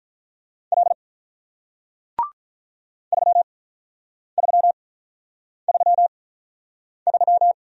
Morse code, heard as a keyed tone, about 48 wpm, spaced as for 9 wpm, 700 Hz. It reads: L 3333